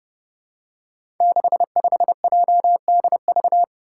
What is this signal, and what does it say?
Morse code, heard as a keyed tone, about 30 wpm, 700 Hz. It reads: 65JDV
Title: Five Challenging Letter-Number Combinations 30wpm